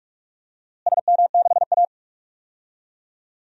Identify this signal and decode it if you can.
Morse code, heard as a keyed tone, about 44 wpm, 700 Hz. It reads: SM6A
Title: DXCC-100 Call Signs - 4 Characters - Repeated 3x - 45wpm